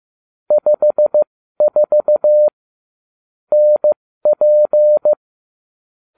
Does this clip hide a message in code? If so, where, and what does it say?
Yes: Morse code from 0.5 to 5.1 s, heard as a keyed tone, about 15 wpm, spaced as for 13 wpm, 600 Hz. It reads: 54 NP